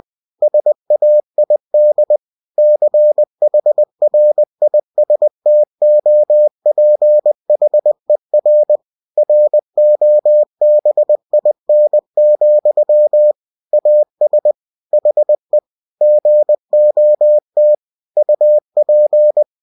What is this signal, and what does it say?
Morse code, heard as a keyed tone, about 20 wpm, 600 Hz. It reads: SAID CHRISTOPHER ROBIN, AS HE GOT UP
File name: cw-books / winnie-the-pooh / winnie-20wpm-page-0076